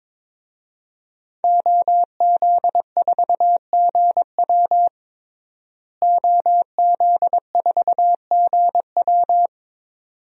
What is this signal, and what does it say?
Morse code, heard as a keyed tone, about 22 wpm, 700 Hz. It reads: OZ4GW OZ4GW
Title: DXCC-100 Call Signs - 5 Characters - Repeated 2x - 22wpm